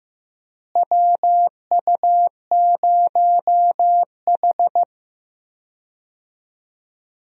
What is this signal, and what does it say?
Morse code, heard as a keyed tone, about 15 wpm, 700 Hz. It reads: WU0H
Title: USA Call Signs - 4 Characters - Repeated 2x - 15wpm